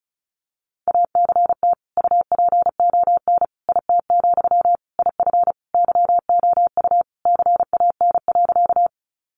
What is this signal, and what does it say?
Morse code, heard as a keyed tone, about 34 wpm, 700 Hz. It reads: ACT UPON IT, IF YOU CAN.
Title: Sentences from Top 1100 Words 35wpm